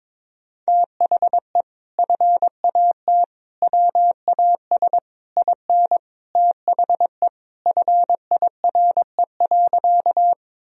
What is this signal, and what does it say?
Morse code, heard as a keyed tone, about 22 wpm, 700 Hz. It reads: THE FAT WAS IN THE FIRE.
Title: Sentences from Top 1000 Words - Review Part 4 - 22wpm